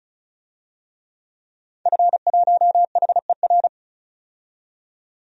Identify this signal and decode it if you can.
Morse code, heard as a keyed tone, about 36 wpm, 700 Hz. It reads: F1HER
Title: DXCC-100 Call Signs - 5 Characters - 35wpm